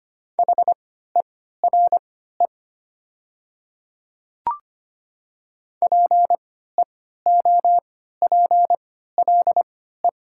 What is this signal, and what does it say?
Morse code, heard as a keyed tone, about 25 wpm, spaced as for 14 wpm, 700 Hz. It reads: HERE PEOPLE
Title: 3x Character Spacing - Top 200 Words - 25wpm